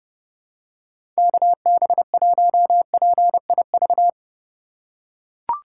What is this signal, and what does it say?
Morse code, heard as a keyed tone, about 31 wpm, 700 Hz. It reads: KB1PIV
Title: USA Call Signs - 6 Characters - 4x Character Spacing - 30wpm